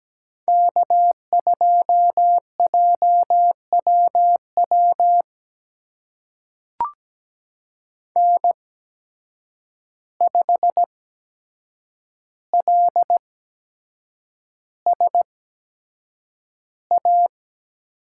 Morse code, heard as a keyed tone, about 17 wpm, spaced as for 7 wpm, 700 Hz. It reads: K2JWW N5LSA
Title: USA Call Signs - 5 Characters - 8x Character Spacing - 17wpm